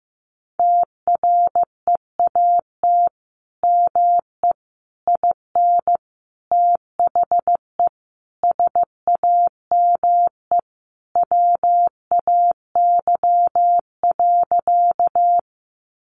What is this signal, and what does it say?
Morse code, heard as a keyed tone, about 15 wpm, 700 Hz. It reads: TREAT ME IN THE SAME WAY.
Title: Sentences from Top 1300 Words 15wpm